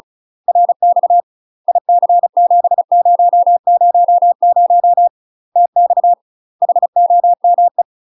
Morse code, heard as a keyed tone, about 34 wpm, 700 Hz. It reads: RX IC7000 TX HOME
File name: UK MASTER QSOs TRAINING / 35WPM / UK MASTER QSOs 0001